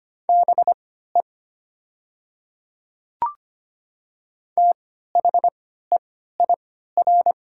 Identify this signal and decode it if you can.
Morse code, heard as a keyed tone, about 25 wpm, spaced as for 14 wpm, 700 Hz. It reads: BE THEIR
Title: 3x Character Spacing - Top 100 Words - 25wpm